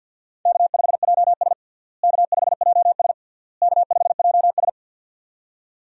Morse code, heard as a keyed tone, about 47 wpm, 700 Hz. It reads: K5JS K5JS K5JS
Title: USA Call Signs - 4 Characters - Repeated 3x - 50wpm